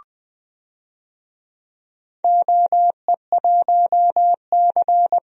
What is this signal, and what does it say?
Morse code, heard as a keyed tone, about 21 wpm, 700 Hz. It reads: OE1C
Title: DXCC-100 Call Signs - 4 Characters - Repeated 2x - 20wpm